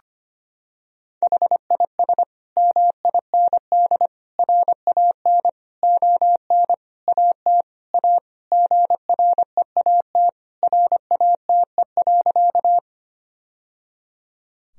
Morse code, heard as a keyed tone, about 25 wpm, 700 Hz. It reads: HIS MIND RAN ON AT A GREAT RATE.
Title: Sentences from Top 1000 Words - Review Part 4 - 25wpm